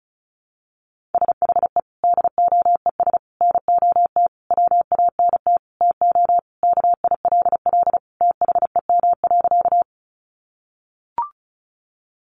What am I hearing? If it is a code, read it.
Morse code, heard as a keyed tone, about 34 wpm, 700 Hz. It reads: SHE DOES NOT WANT TO KILL THEM.
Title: Easy Phrases 35wpm